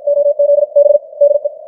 Morse code, heard as a keyed tone, about 53 wpm, spaced as for 46 wpm, 590 Hz. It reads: OZB DE
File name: archived / CWBros